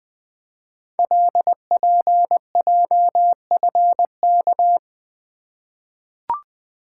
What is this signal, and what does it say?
Morse code, heard as a keyed tone, about 20 wpm, 700 Hz. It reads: LPJFK
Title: Code Groups - 5 Letters - 20wpm